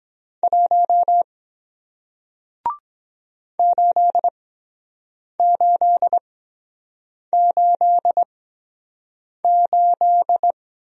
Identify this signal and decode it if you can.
Morse code, heard as a keyed tone, about 24 wpm, 700 Hz. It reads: 1 8 8 8 8